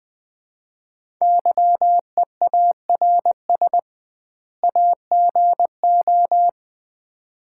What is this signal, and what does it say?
Morse code, heard as a keyed tone, about 20 wpm, 700 Hz. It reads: YEARS AGO
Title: 2x Word-Spacing - Sets of 2 Words - 20wpm